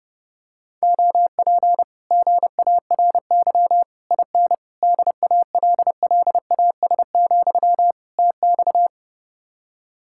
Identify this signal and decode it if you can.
Morse code, heard as a keyed tone, about 30 wpm, 700 Hz. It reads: OP GARY IN DALLAS, TX